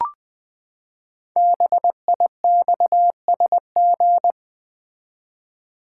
Morse code, heard as a keyed tone, about 21 wpm, 700 Hz. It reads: BIXSG
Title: Code Groups - 5 Letters - 20wpm